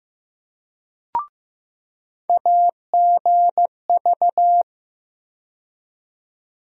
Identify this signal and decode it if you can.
Morse code, heard as a keyed tone, about 15 wpm, 700 Hz. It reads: AGV
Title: USA Call Sign Postfixes - 3 Letters 15wpm